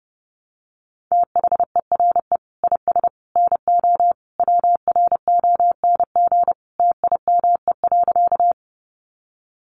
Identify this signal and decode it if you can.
Morse code, heard as a keyed tone, about 30 wpm, 700 Hz. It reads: THERE IS NO WRONG TIME.
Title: Sentences from Top 900 Words 30wpm